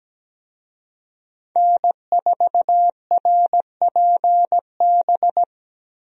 Morse code, heard as a keyed tone, about 17 wpm, 700 Hz. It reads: N4RPB
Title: USA Call Signs - 5 Characters - Repeated 3x - 17wpm